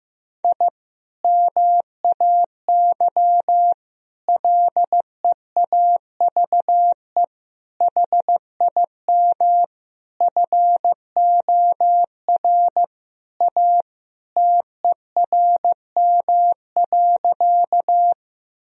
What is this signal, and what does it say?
Morse code, heard as a keyed tone, about 15 wpm, 700 Hz. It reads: I MAY LEAVE HIM FOR A TERM.
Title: Sentences from Top 1000 Words - Review Part 4 - 15wpm